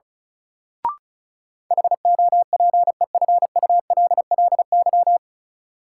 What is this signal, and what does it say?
Morse code, heard as a keyed tone, about 36 wpm, 700 Hz. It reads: HOPEFULLY
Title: Sentences from Top 1500 Words - New Word Drill - 35wpm